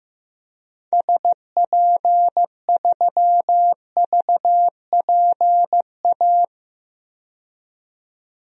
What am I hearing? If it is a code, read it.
Morse code, heard as a keyed tone, about 15 wpm, 700 Hz. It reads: SP3VPA
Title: DXCC-100 Call Signs - 6 Characters - 15wpm